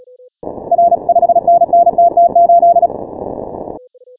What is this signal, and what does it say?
Morse code, heard as a keyed tone, about 36 wpm, 690 Hz. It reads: R 5NN TT8
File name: final / W1UJ-168-140-MorseRunner